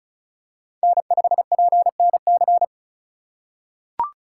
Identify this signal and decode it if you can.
Morse code, heard as a keyed tone, about 36 wpm, 700 Hz. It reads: N5PNC
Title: USA Call Signs - 5 Characters - 4x Character Spacing - 35wpm